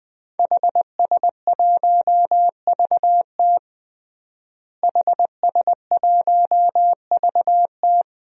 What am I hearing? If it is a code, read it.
Morse code, heard as a keyed tone, about 21 wpm, 700 Hz. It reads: HS1VT HS1VT
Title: DXCC-100 Call Signs - 5 Characters - Repeated 3x - 20wpm